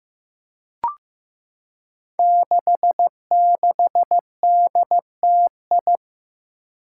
Morse code, heard as a keyed tone, about 15 wpm, 700 Hz. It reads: 66DTI
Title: Five Challenging Letter-Number Combinations 15wpm